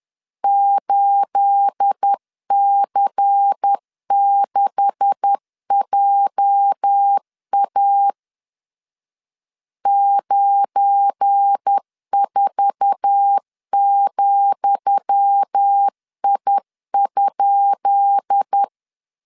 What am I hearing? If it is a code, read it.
Morse code, heard as a keyed tone, about 10 wpm, 800 Hz. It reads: 8C6JA 94,I?
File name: LEZIONI / 10 SIGNS / 10D COMA 10wpm 3013